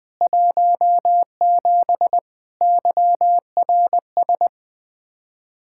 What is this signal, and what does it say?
Morse code, heard as a keyed tone, about 20 wpm, 700 Hz. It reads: 17 YRS